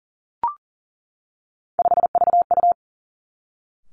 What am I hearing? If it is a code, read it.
Morse code, heard as a keyed tone, about 39 wpm, 700 Hz. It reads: 5VU
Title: Three Challenging Letter-Number Combinations 40wpm